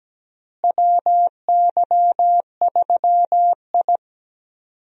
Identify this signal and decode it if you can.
Morse code, heard as a keyed tone, about 17 wpm, 700 Hz. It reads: WY3I
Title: USA Call Signs - 4 Characters - Repeated 3x - 17wpm